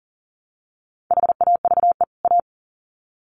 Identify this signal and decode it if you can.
Morse code, heard as a keyed tone, about 40 wpm, 700 Hz. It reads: HAVE A
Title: Sets of 2 Words 40wpm